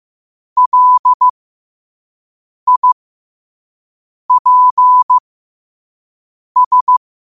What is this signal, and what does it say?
Morse code, heard as a keyed tone, about 15 wpm, spaced as for 5 wpm, 1000 Hz. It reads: LIPS
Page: New quotes every day in morse code at 5 Words per minute.